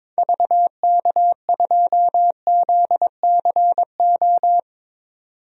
Morse code, heard as a keyed tone, about 22 wpm, 700 Hz. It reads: VK2ZCO
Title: DXCC-100 Call Signs - 6 Characters - 22wpm